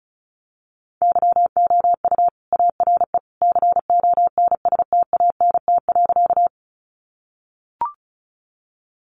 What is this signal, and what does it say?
Morse code, heard as a keyed tone, about 34 wpm, 700 Hz. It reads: YOU ARE CONSTANT.
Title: Sentences from Top 1500 Words 35wpm